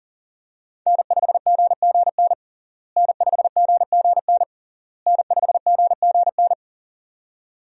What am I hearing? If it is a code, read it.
Morse code, heard as a keyed tone, about 40 wpm, 700 Hz. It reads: N5GGN N5GGN N5GGN
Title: USA Call Signs - 5 Characters - Repeated 3x - 40wpm